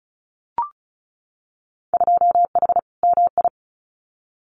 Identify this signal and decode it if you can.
Morse code, heard as a keyed tone, about 36 wpm, 700 Hz. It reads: 2H MI
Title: Winter Field Day Exchanges 35wpm